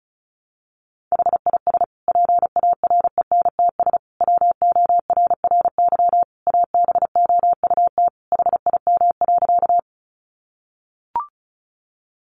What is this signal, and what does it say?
Morse code, heard as a keyed tone, about 34 wpm, 700 Hz. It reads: HIS PARENTS WORRY ABOUT HIM.
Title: Easy Phrases 35wpm